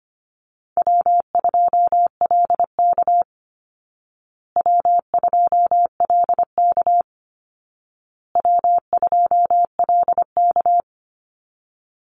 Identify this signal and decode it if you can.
Morse code, heard as a keyed tone, about 25 wpm, 700 Hz. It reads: W2LK W2LK W2LK
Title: Call Signs Repeated 2x - 25wpm